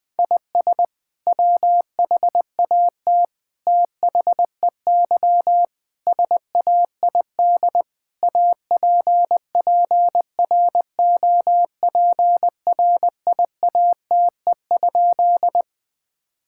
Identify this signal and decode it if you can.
Morse code, heard as a keyed tone, about 20 wpm, 700 Hz. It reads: IS WHAT THEY SAID APPROPRIATE?
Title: Sentences from Top 1000 Words - Review Part 5 - 20wpm